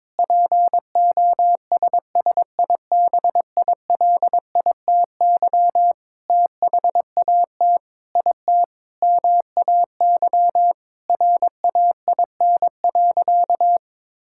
Morse code, heard as a keyed tone, about 22 wpm, 700 Hz. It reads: POSSIBILITY THAT IT MAY RAIN.
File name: practice / 22wpm / Sentences with 8-Word Phrases - 22wpm